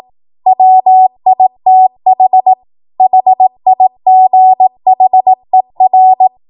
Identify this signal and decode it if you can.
Morse code, heard as a keyed tone, about 18 wpm, 750 Hz. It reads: WITH HIGHER
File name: Morse / 18WPM